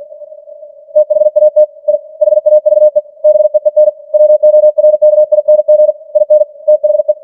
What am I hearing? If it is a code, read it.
Morse code, heard as a keyed tone, about 50 wpm, 610 Hz. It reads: THAT I HAVE BEEN WORKING IN THE
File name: archived / CWBros